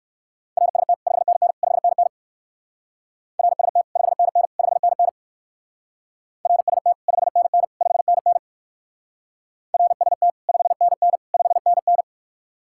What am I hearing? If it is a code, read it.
Morse code, heard as a keyed tone, about 59 wpm, 700 Hz. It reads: RST 5NN 5NN RST 5NN 5NN RST 5NN 5NN RST 5NN 5NN